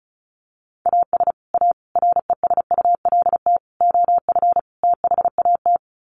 Morse code, heard as a keyed tone, about 34 wpm, 700 Hz. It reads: AS A RESULT OF THAT
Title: Sets of 5 Words Encore 35wpm